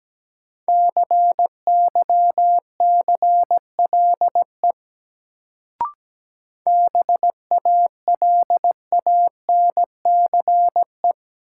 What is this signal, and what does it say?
Morse code, heard as a keyed tone, about 17 wpm, 700 Hz. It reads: CYCLE BALANCE
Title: Sentences from Top 900 Words - New Word Drill - 17wpm